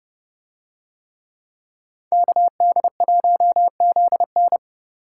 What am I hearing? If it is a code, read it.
Morse code, heard as a keyed tone, about 30 wpm, 700 Hz. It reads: KD1ZN